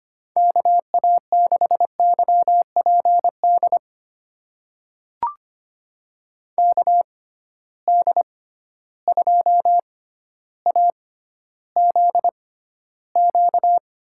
Morse code, heard as a keyed tone, about 25 wpm, spaced as for 13 wpm, 700 Hz. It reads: KA6YPD KD2AZQ
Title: USA Call Signs - 6 Characters - 6x Character Spacing - 25wpm